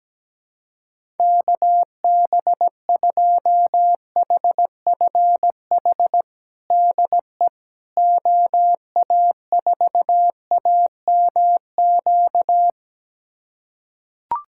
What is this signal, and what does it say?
Morse code, heard as a keyed tone, about 17 wpm, 700 Hz. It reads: KB2HFH DE OA4AMQ
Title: USA Callsign DE Callsign - 6 Characters - 17wpm